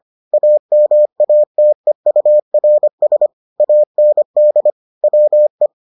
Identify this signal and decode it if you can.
Morse code, heard as a keyed tone, about 25 wpm, 600 Hz. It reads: AMATEURS AND WE